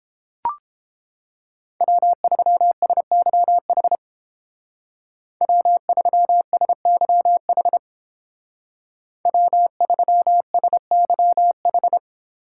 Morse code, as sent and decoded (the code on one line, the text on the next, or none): .-- ...-- ... -.-- .... / .-- ...-- ... -.-- .... / .-- ...-- ... -.-- ....
W3SYH W3SYH W3SYH